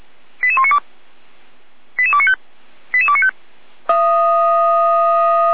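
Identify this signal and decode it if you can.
Five-tone selective call (ZVEI), heard as quick runs of five tones, tones of about 70 ms. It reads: 88192 88295 88295
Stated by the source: ZVEI five-tone selective call received by SDR, fire brigade Orlamünde (feuerwehr-orlamuende), address 88102.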